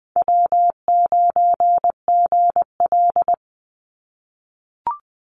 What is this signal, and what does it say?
Morse code, heard as a keyed tone, about 20 wpm, 700 Hz. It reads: W9GL
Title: Easy Call Signs - 20wpm